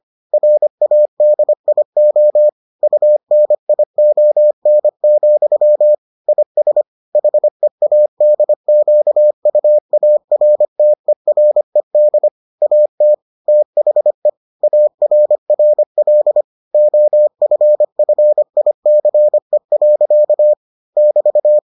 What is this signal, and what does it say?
Morse code, heard as a keyed tone, about 25 wpm, 600 Hz. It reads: RADIO UNION, IS HEADQUARTERED AT THE ARRL OFFICE. <BT>